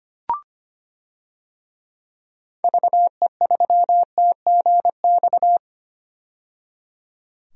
Morse code, heard as a keyed tone, about 25 wpm, 700 Hz. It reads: VE3TGX
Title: DXCC-100 Call Signs - 6 Characters - 25wpm